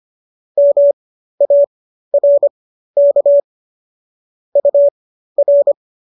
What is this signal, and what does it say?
Morse code, heard as a keyed tone, about 25 wpm, spaced as for 13 wpm, 570 Hz. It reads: MARK UR
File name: qso206 13